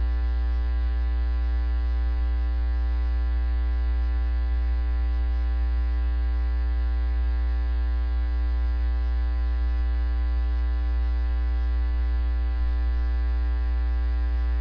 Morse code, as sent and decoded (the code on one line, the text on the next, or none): none